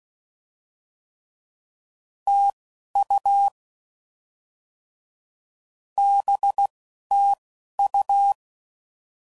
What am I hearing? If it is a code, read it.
Morse code, heard as a keyed tone, about 16 wpm, spaced as for 11 wpm, 800 Hz. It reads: TU BTU
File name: Mixed / Misc / Words